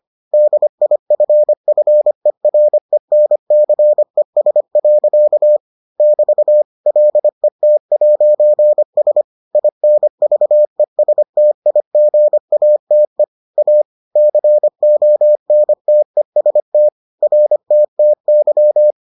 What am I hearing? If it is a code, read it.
Morse code, heard as a keyed tone, about 25 wpm, 600 Hz. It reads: DIFFERENCES. <BT> LET'S INVESTIGATE A CONTEST RTTY